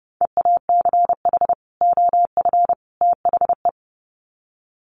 Morse code, heard as a keyed tone, about 30 wpm, 700 Hz. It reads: EACH OF THE
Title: Sets of 3 Words 30wpm